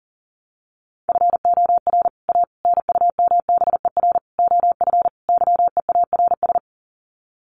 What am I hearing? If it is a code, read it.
Morse code, heard as a keyed tone, about 40 wpm, 700 Hz. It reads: FOR A NUMBER OF YEARS
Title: Sets of 5 Words Encore 40wpm